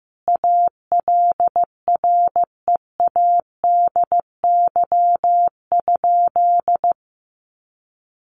Morse code, heard as a keyed tone, about 15 wpm, 700 Hz. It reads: ALREADY?
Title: Sentences from Top 700 Words 15wpm